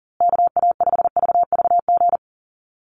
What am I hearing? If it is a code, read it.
Morse code, heard as a keyed tone, about 40 wpm, 700 Hz. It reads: KA5VVG
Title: USA Call Signs Repeated 3x 40wpm